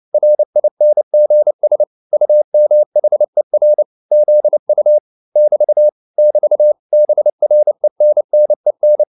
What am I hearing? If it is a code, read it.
Morse code, heard as a keyed tone, about 29 wpm, 600 Hz. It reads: RINGS UMHER ZU <BT> <BT> BRENNEN